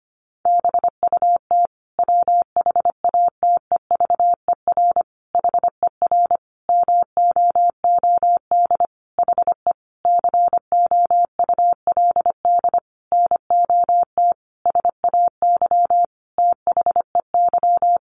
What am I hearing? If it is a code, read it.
Morse code, heard as a keyed tone, about 25 wpm, 700 Hz. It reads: BUT WHATEVER HER MOOD HE COULD NOT SAY THEY
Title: Sentences from Top 1100 Words 25wpm